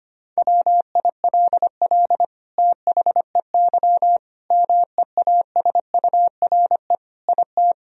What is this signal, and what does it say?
Morse code, heard as a keyed tone, about 25 wpm, 700 Hz. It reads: WILL THEY MEASURE IT
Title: Sentences from Top 1000 Words - Review Part 2 - 25wpm